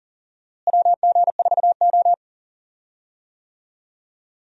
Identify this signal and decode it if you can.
Morse code, heard as a keyed tone, about 40 wpm, 700 Hz. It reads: WG4O
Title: USA Call Signs - 4 Characters - Repeated 2x - 40wpm